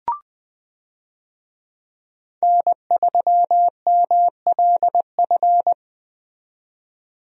Morse code, heard as a keyed tone, about 20 wpm, 700 Hz. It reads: N3MLF
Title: USA Call Signs - 5 Characters - 20wpm